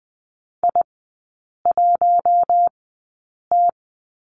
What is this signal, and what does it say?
Morse code, heard as a keyed tone, about 20 wpm, spaced as for 8 wpm, 700 Hz. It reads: I1T
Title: 041 Lesson 8e - Number 1 - Three Characters 20wpm